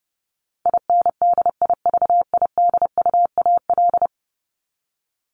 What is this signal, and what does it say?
Morse code, heard as a keyed tone, about 28 wpm, 700 Hz. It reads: INDIVIDUAL
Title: ES:S Suffix - Top 100 Words with Base Words 30wpm